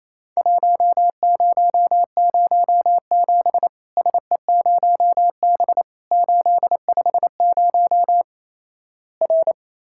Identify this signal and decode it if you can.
Morse code, heard as a keyed tone, about 28 wpm, 700 Hz. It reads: 1007 SE0B 850 R
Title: Worked All Europe DX Contest - QTC Exchange - 28wpm